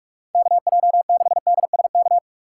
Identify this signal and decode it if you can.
Morse code, heard as a keyed tone, about 41 wpm, 700 Hz. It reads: KJ6DSK
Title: USA Call Signs - 6 Characters - 3x Character Spacing - 45wpm